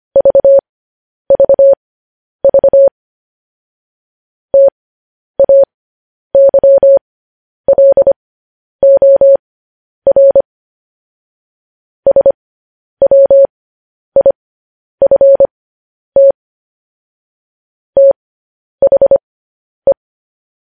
Morse code, heard as a keyed tone, about 25 wpm, spaced as for 10 wpm, 550 Hz. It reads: VVV TAYLOR SWIFT THE
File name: podcast / News headlines-1x-25x10